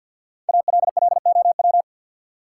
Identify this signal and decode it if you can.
Morse code, heard as a keyed tone, about 52 wpm, 700 Hz. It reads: ALLOW
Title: Sentences from Top 700 Words - New Word Drill - 50wpm